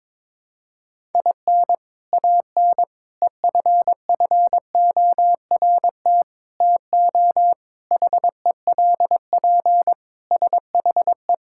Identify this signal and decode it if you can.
Morse code, heard as a keyed tone, about 22 wpm, 700 Hz. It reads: IN AN EFFORT TO HELP SHE
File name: practice / 22wpm / Sentences with 4-Word Phrases - 22wpm